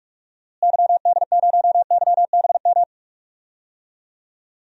Morse code, heard as a keyed tone, about 46 wpm, 700 Hz. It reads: YD0YBM
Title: DXCC-100 Call Signs - 6 Characters - Repeated 3x - 45wpm